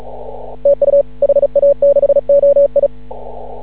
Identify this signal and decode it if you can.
Morse code, heard as a keyed tone, about 36 wpm, 560 Hz. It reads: TU HA6OI
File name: CW / ha6oi 2